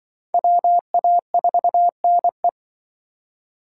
Morse code, heard as a keyed tone, about 24 wpm, 700 Hz. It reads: WA4NE
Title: Speed Racing - USA Call Signs 20wpm